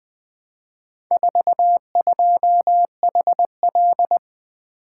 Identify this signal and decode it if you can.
Morse code, heard as a keyed tone, about 20 wpm, 700 Hz. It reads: 42HL
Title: Four Letter-Number Combinations - 20wpm